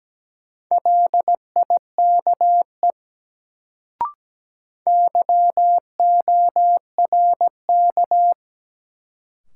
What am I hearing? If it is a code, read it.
Morse code, heard as a keyed tone, about 17 wpm, 700 Hz. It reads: LIKE YORK
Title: Sets of 2 Words - New Word Drill - 17wpm